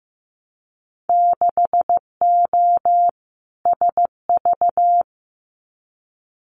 Morse code, heard as a keyed tone, about 15 wpm, 700 Hz. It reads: 6O SV
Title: Winter Field Day Exchanges Encore 15wpm